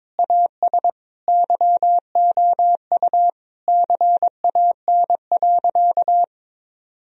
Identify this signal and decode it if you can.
Morse code, heard as a keyed tone, about 22 wpm, 700 Hz. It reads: AS YOU CAN.
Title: Sentences from Top 1000 Words - Review Part 3 - 22wpm